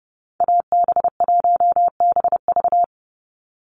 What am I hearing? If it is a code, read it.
Morse code, heard as a keyed tone, about 29 wpm, 700 Hz. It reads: AB1BV
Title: USA Call Signs 30wpm